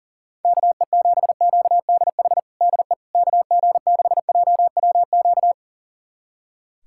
Morse code, heard as a keyed tone, about 40 wpm, 700 Hz. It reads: KE7QDH DE KG6JWQ
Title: USA Callsign DE Callsign - 6 Characters - 40wpm